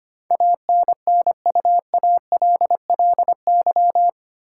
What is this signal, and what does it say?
Morse code, heard as a keyed tone, about 25 wpm, 700 Hz. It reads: ANNUALLY
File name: 8 Letter Words - First 3 Letters Given - 25wpm